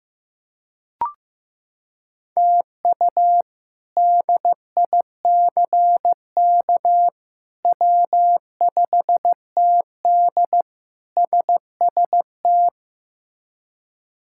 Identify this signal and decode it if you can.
Morse code, heard as a keyed tone, about 15 wpm, 700 Hz. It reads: TU DICK W5TD SST
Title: K1USN SST QSO Practice - 15wpm